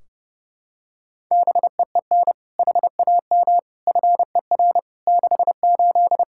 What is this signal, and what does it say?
Morse code, heard as a keyed tone, about 30 wpm, 700 Hz. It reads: BEEN HAM FER 68
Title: Practice QSO - Exchange 3 - More Info - 30wpm